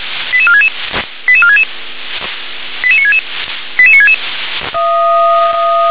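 Five-tone selective call (ZVEI), heard as quick runs of five tones, tones of about 70 ms. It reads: 88466 88466 88966 88966
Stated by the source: Five-tone ZVEI selective call received on SDR.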